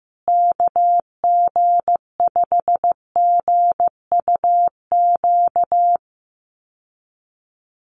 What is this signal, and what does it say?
Morse code, heard as a keyed tone, about 15 wpm, 700 Hz. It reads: KG5GUQ
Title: USA Call Signs Repeated 3x 15wpm